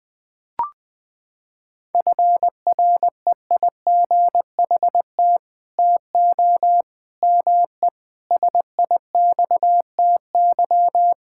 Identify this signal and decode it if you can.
Morse code, heard as a keyed tone, about 21 wpm, 700 Hz. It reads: FREIGHT TO ME SIXTY
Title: Pangram Challenge v2 - 20wpm